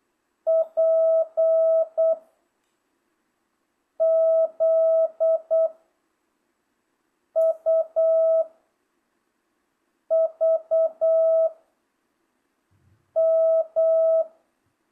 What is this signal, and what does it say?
Morse code, heard as a keyed tone, about 8 wpm, 660 Hz. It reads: P Z U V M